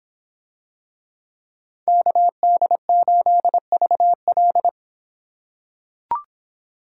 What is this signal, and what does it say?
Morse code, heard as a keyed tone, about 25 wpm, 700 Hz. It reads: KD8VL